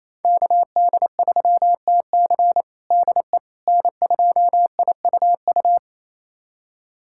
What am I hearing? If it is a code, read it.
Morse code, heard as a keyed tone, about 28 wpm, 700 Hz. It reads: KD3TC DE N2IUU
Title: USA Callsign DE Callsign - 5 Characters - 28wpm